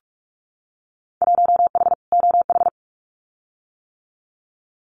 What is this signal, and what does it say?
Morse code, heard as a keyed tone, about 44 wpm, 700 Hz. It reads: 1H OH
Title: Winter Field Day Exchanges 45wpm